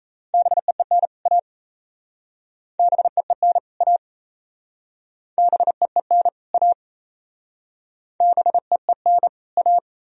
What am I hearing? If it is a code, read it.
Morse code, heard as a keyed tone, about 37 wpm, 700 Hz. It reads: BEEN A BEEN A BEEN A BEEN A